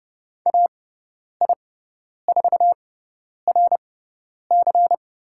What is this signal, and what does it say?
Morse code, heard as a keyed tone, about 30 wpm, spaced as for 10 wpm, 700 Hz. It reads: AI4RC
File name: Lesson 18g - Letter m - Call Signs 30wpm